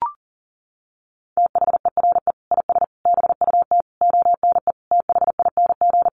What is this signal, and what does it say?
Morse code, heard as a keyed tone, about 40 wpm, 700 Hz. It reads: THERE IS BUT ONE THING
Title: Sentences from Top 100 Words 40wpm